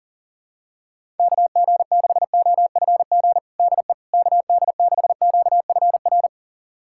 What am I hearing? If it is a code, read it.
Morse code, heard as a keyed tone, about 40 wpm, 700 Hz. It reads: KG6OFG DE KD6QFR
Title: USA Callsign DE Callsign 40wpm